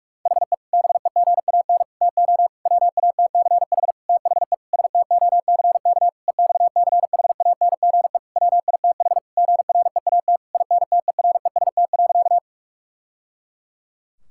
Morse code, heard as a keyed tone, about 41 wpm, 700 Hz. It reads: HE BEGAN TO WATCH THE STOCK EXCHANGE WITH GREAT INTEREST.
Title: Sentences from Top 1000 Words - Review Part 6 - 45wpm